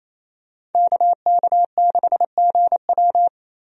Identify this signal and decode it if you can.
Morse code, heard as a keyed tone, about 27 wpm, 700 Hz. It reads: KK6GW
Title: USA Call Signs - 5 Characters - Repeated 3x - 28wpm